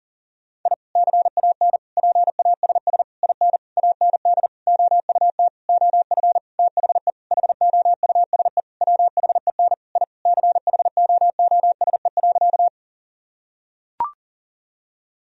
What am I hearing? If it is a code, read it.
Morse code, heard as a keyed tone, about 37 wpm, 700 Hz. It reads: I CAN PASS IN AND OUT OF THE HOUSE WHEN I CHOOSE.
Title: Sentences from Top 1000 Words - Review Part 4 - 40wpm